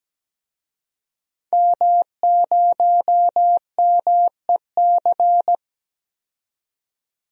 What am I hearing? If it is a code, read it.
Morse code, heard as a keyed tone, about 17 wpm, 700 Hz. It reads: M0MEC
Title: DXCC-100 Call Signs - 5 Characters - 17wpm